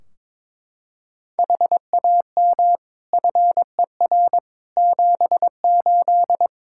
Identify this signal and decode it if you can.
Morse code, heard as a keyed tone, about 22 wpm, 700 Hz. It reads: HAM FER 78
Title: Practice QSO - Exchange 3 - More Info - 22wpm